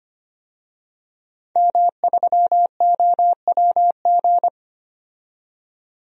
Morse code, heard as a keyed tone, about 25 wpm, 700 Hz. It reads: M3OWG
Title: DXCC-100 Call Signs - 5 Characters - 25wpm